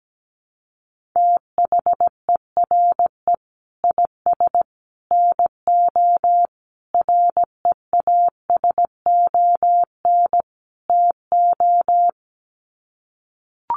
Morse code, heard as a keyed tone, about 17 wpm, 700 Hz. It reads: THERE IS NO REASON TO
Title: Sets of 5 Words Encore 17wpm